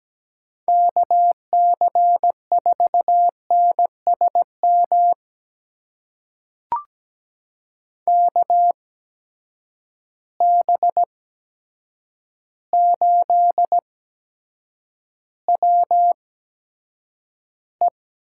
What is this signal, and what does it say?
Morse code, heard as a keyed tone, about 17 wpm, spaced as for 8 wpm, 700 Hz. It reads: KC4NSM KB8WE
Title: USA Call Signs - 6 Characters - 8x Character Spacing - 17wpm